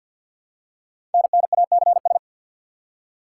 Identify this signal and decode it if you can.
Morse code, heard as a keyed tone, about 46 wpm, 700 Hz. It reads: NNACS
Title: Code Groups - 5 Letters - 50wpm